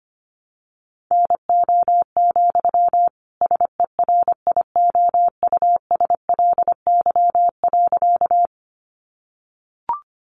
Morse code, heard as a keyed tone, about 25 wpm, 700 Hz. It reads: NO, SERIOUSLY.
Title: Sentences from Top 1200 Words 25wpm